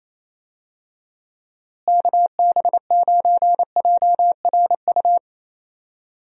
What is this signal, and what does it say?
Morse code, heard as a keyed tone, about 28 wpm, 700 Hz. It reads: KB9JRU
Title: USA Call Signs - 6 Characters - Repeated 2x - 28wpm